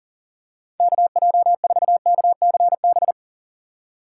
Morse code, heard as a keyed tone, about 40 wpm, 700 Hz. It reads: KJ4KCB
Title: USA Call Signs - 6 Characters - 2x Character Spacing - 40wpm